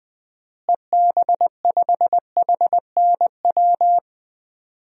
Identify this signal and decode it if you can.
Morse code, heard as a keyed tone, about 20 wpm, 700 Hz. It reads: EB5HNW
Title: DXCC-100 Call Signs - 6 Characters - Repeated 2x - 20wpm